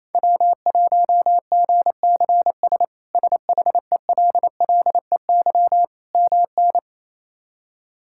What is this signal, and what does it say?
Morse code, heard as a keyed tone, about 28 wpm, 700 Hz. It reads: W1GCS SHELLEY MN
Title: K1USN SST QSO Practice - 28wpm